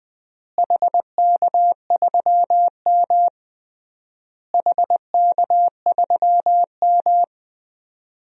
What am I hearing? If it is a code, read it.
Morse code, heard as a keyed tone, about 20 wpm, 700 Hz. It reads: HK3M HK3M
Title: DXCC-100 Call Signs - 4 Characters - Repeated 3x - 20wpm